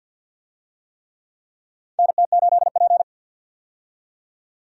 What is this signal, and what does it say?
Morse code, heard as a keyed tone, about 52 wpm, 700 Hz. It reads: NT8P
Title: USA Call Signs - 4 Characters - 50wpm